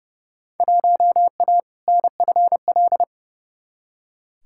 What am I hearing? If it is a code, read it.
Morse code, heard as a keyed tone, about 30 wpm, 700 Hz. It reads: 1A NFL
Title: Field Day Exchanges - 30wpm